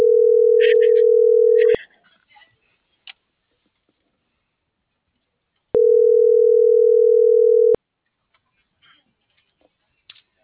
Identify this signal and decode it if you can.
Ringback tone (North America), 440 + 480 Hz.